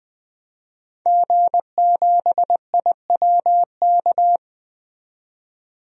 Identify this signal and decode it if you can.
Morse code, heard as a keyed tone, about 20 wpm, 700 Hz. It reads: G7IWK
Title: DXCC-100 Call Signs - 5 Characters - 20wpm